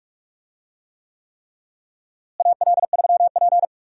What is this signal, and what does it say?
Morse code, heard as a keyed tone, about 43 wpm, 700 Hz. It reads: AL3P
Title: USA Call Signs - 4 Characters - Repeated 3x - 45wpm